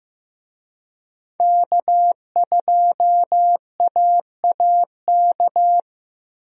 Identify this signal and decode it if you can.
Morse code, heard as a keyed tone, about 15 wpm, 700 Hz. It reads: K2AAK
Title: USA Call Signs - 5 Characters - 15wpm